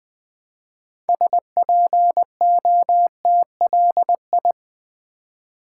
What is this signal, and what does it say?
Morse code, heard as a keyed tone, about 20 wpm, 700 Hz. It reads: SPOTLI
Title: Build A Word - 3 Letters At A Time - 9-Letter Words - 20wpm